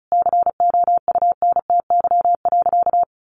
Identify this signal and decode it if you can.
Morse code, heard as a keyed tone, about 34 wpm, 700 Hz. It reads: COUNTY.
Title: Sentences from Top 1200 Words 35wpm